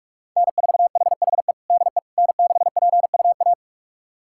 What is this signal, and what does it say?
Morse code, heard as a keyed tone, about 46 wpm, 700 Hz. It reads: N4HHE DE N6PUA
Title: USA Callsign DE Callsign - 5 Characters - 45wpm